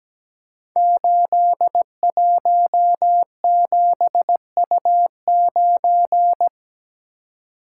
Morse code, heard as a keyed tone, about 17 wpm, 700 Hz. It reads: 817U9
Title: Code Groups - 5 Letter:Numbers - 17wpm